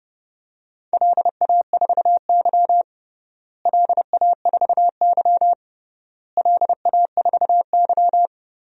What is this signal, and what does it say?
Morse code, heard as a keyed tone, about 30 wpm, 700 Hz. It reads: LA4Y LA4Y LA4Y
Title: DXCC-100 Call Signs - 4 Characters - Repeated 3x - 30wpm